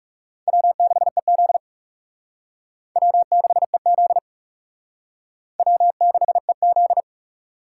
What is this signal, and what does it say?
Morse code, heard as a keyed tone, about 38 wpm, 700 Hz. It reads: W6EZ W6EZ W6EZ